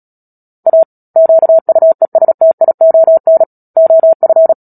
Morse code, heard as a keyed tone, about 37 wpm, 650 Hz. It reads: A QUESTION OF
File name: Book Two / 10-Chapter 10 - Epilogue - 36 WPM